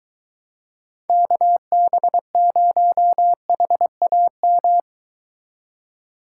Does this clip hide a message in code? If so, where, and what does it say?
Yes: Morse code from 1.1 to 4.8 s, heard as a keyed tone, about 23 wpm, 700 Hz. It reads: KB0HAM